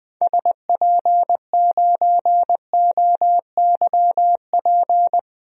Morse code, heard as a keyed tone, about 20 wpm, 700 Hz. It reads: SP9OYP
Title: DXCC-100 Call Signs - 6 Characters - 20wpm